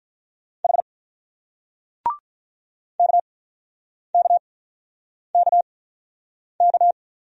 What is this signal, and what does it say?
Morse code, heard as a keyed tone, about 44 wpm, spaced as for 9 wpm, 700 Hz. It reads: H KKKK